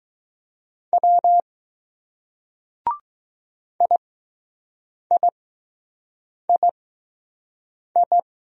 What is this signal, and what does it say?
Morse code, heard as a keyed tone, about 20 wpm, spaced as for 6 wpm, 700 Hz. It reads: W IIII